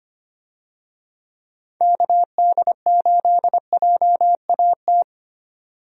Morse code, heard as a keyed tone, about 25 wpm, 700 Hz. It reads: KD8JAT
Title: USA Call Signs - 6 Characters - 25wpm